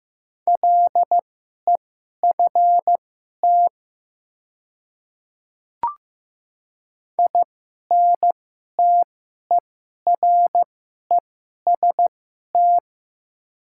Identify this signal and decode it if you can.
Morse code, heard as a keyed tone, about 15 wpm, spaced as for 11 wpm, 700 Hz. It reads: LEFT INTEREST
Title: 2x Character Spacing - Top 500 Words - 15wpm